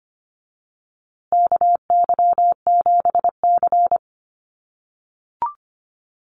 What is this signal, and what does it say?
Morse code, heard as a keyed tone, about 25 wpm, 700 Hz. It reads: KY7C
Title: USA Call Signs Repeated 3x 25wpm